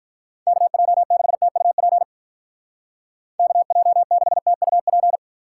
Morse code, heard as a keyed tone, about 43 wpm, 700 Hz. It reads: KJ6TUP KJ6TUP
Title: Speed Racing - USA 2x3 Call Signs 35wpm